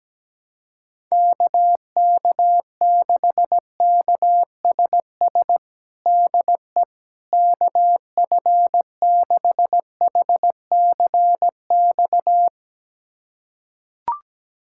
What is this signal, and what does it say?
Morse code, heard as a keyed tone, about 17 wpm, 700 Hz. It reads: KK6KSS DE KF6HCX
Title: USA Callsign DE Callsign - 6 Characters - 17wpm